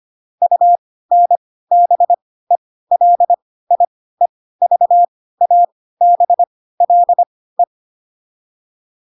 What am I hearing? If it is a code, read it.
Morse code, heard as a keyed tone, about 25 wpm, spaced as for 16 wpm, 700 Hz. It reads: UNBELIEVABLE
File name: Pinocchio / Pinocchio ch3 10 25 ik1hge